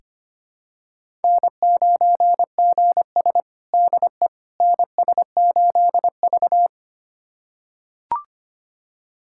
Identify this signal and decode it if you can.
Morse code, heard as a keyed tone, about 25 wpm, 700 Hz. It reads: N9GS DE NS8V
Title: USA Callsign DE Callsign - 4 Characters - 25wpm